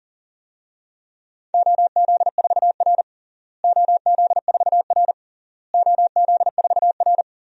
Morse code, heard as a keyed tone, about 40 wpm, 700 Hz. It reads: OZ4R OZ4R OZ4R
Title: DXCC-100 Call Signs - 4 Characters - Repeated 3x - 40wpm